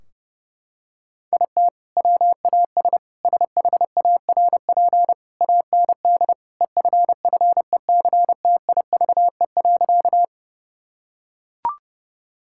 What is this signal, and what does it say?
Morse code, heard as a keyed tone, about 32 wpm, 700 Hz. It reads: IT WAS SHARP AND EFFECTIVE.
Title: Sentences from Top 1600 Words 30wpm